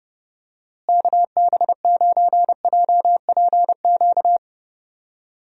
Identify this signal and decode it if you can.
Morse code, heard as a keyed tone, about 30 wpm, 700 Hz. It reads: KB9JPQ
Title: USA Call Signs - 6 Characters - Repeated 3x - 30wpm